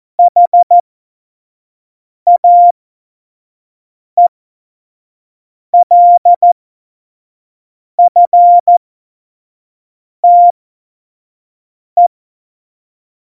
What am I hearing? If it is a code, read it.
Morse code, heard as a keyed tone, about 14 wpm, spaced as for 5 wpm, 700 Hz. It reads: HAELFTE
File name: Morsekurs / Witze / Witz 22. Januar 2019, 25 F70